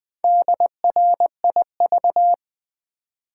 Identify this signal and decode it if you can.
Morse code, heard as a keyed tone, about 20 wpm, 700 Hz. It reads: DRIV
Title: Build A Word - 2 Letters At A Time - 6-Letter Words - 20wpm